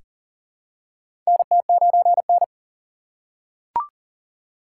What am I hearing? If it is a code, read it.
Morse code, heard as a keyed tone, about 39 wpm, 700 Hz. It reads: NT9N
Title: USA Call Signs - 4 Characters - 3x Character Spacing - 40wpm